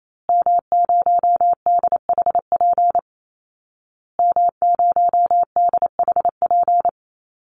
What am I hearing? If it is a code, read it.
Morse code, heard as a keyed tone, about 28 wpm, 700 Hz. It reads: M0DHP M0DHP
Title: Call Signs Repeated 2x - 28wpm